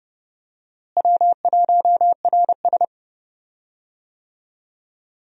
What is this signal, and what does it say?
Morse code, heard as a keyed tone, about 30 wpm, 700 Hz. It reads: W1RS
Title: USA Call Signs - 4 Characters - Repeated 3x - 30wpm